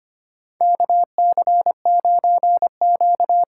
Morse code, heard as a keyed tone, about 25 wpm, 700 Hz. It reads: KC9Q
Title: USA Call Signs - 6 Characters - 6x Character Spacing - 25wpm